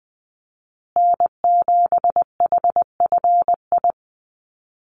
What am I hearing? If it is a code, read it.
Morse code, heard as a keyed tone, about 20 wpm, 700 Hz. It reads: N7HFI
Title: USA Call Signs Repeated 2x 20wpm